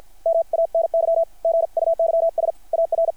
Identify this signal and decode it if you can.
Morse code, heard as a keyed tone, about 50 wpm, 660 Hz. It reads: MANY GUYS AR
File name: archived / CRSnet